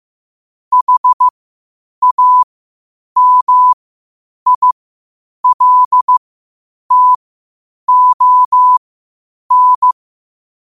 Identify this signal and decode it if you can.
Morse code, heard as a keyed tone, about 15 wpm, spaced as for 8 wpm, 1000 Hz. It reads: HAMILTON
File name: qotdfiles / 8wpm